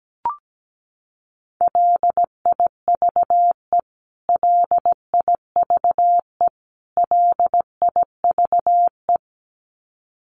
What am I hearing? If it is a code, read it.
Morse code, heard as a keyed tone, about 17 wpm, 700 Hz. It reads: LIVE LIVE LIVE
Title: Top 500 Words Repeated 17wpm